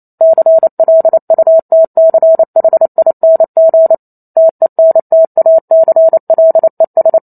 Morse code, heard as a keyed tone, about 28 wpm, 650 Hz. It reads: CLUTCHING TENTACLES